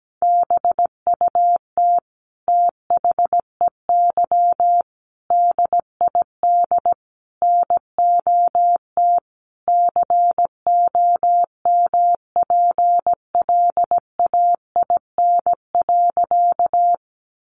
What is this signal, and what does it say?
Morse code, heard as a keyed tone, about 17 wpm, 700 Hz. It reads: BUT THEY DID NOT COMPLAIN.
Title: Sentences from Top 1300 Words 17wpm